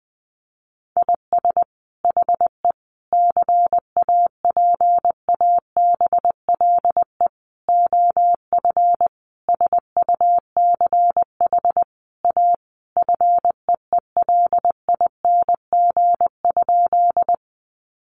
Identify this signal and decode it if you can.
Morse code, heard as a keyed tone, about 20 wpm, 700 Hz. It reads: IS HE CAPABLE OF SUCH A FEELING?
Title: Sentences from Top 1200 Words 20wpm